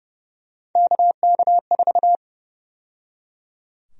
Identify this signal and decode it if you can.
Morse code, heard as a keyed tone, about 29 wpm, 700 Hz. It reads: KK4
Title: USA Call Sign Prefixes - 30wpm